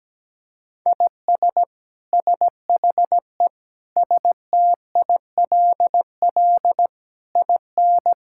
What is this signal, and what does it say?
Morse code, heard as a keyed tone, about 17 wpm, 700 Hz. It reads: IS SHE STILL IN
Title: Sentences from Top 1000 Words - Review Part 5 - 17wpm